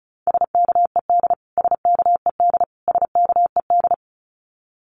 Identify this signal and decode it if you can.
Morse code, heard as a keyed tone, about 34 wpm, 700 Hz. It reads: SKED SKED SKED
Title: Abbreviations Repeated 3x - 35wpm